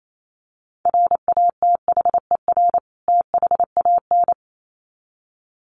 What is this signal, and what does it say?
Morse code, heard as a keyed tone, about 28 wpm, 700 Hz. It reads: RATHER THAN
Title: Sets of 2 Words 28wpm